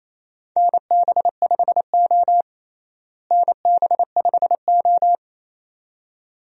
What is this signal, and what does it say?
Morse code, heard as a keyed tone, about 28 wpm, 700 Hz. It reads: NB5O NB5O
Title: USA Call Signs - 4 Characters - Repeated 2x - 28wpm